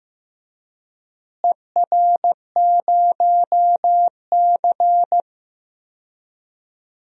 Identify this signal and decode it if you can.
Morse code, heard as a keyed tone, about 15 wpm, 700 Hz. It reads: ER0C
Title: DXCC-100 Call Signs - 4 Characters - Repeated 3x - 15wpm